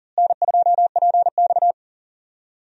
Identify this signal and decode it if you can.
Morse code, heard as a keyed tone, about 38 wpm, 700 Hz. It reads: N2PX
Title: USA Call Signs - 4 Characters - Repeated 3x - 40wpm